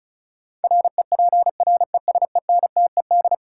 Morse code, heard as a keyed tone, about 36 wpm, 700 Hz. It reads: REPRESENTED
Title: Sets of 5 Words - New Word Drill - 35wpm